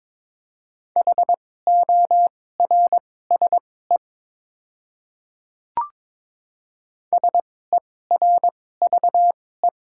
Morse code, heard as a keyed tone, about 21 wpm, spaced as for 16 wpm, 700 Hz. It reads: HORSE SERVE
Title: 2x Character Spacing - Top 500 Words - 22wpm